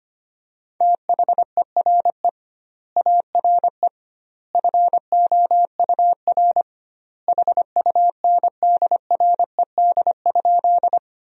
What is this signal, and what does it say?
Morse code, heard as a keyed tone, about 25 wpm, 700 Hz. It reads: THERE ARE FOUR HUNDRED?
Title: 2x Word-Spacing - Sentences from Top 400 Words - 25wpm